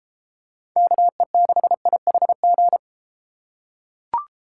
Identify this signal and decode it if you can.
Morse code, heard as a keyed tone, about 32 wpm, 700 Hz. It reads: KE6IHG